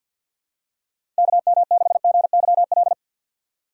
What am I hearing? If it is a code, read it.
Morse code, heard as a keyed tone, about 52 wpm, 700 Hz. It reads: KM6GYL
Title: USA Call Signs - 3x Character Spacing - 50wpm